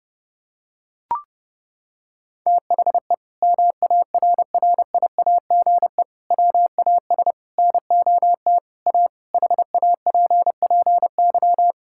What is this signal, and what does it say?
Morse code, heard as a keyed tone, about 30 wpm, 700 Hz. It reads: THE MARRIAGE WAS NOT A HAPPY
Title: Sentences from Top 1000 Words - Review Part 5 - 30wpm